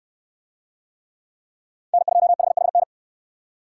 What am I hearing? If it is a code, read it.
Morse code, heard as a keyed tone, about 68 wpm, 700 Hz. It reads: N2HHN